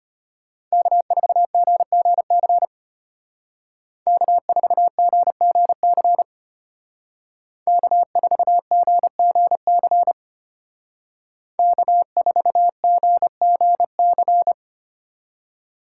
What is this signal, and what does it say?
Morse code, heard as a keyed tone, about 32 wpm, 700 Hz. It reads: K4GGC K4GGC K4GGC K4GGC